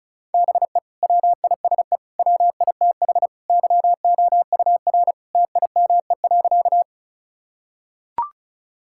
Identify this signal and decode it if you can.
Morse code, heard as a keyed tone, about 34 wpm, 700 Hz. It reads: BE WISE WITH YOUR TIME.
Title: Sentences from Top 1600 Words 35wpm